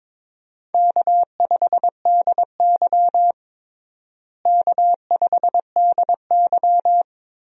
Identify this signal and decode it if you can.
Morse code, heard as a keyed tone, about 22 wpm, 700 Hz. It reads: K5DY K5DY
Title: USA Call Signs - 4 Characters - Repeated 3x - 22wpm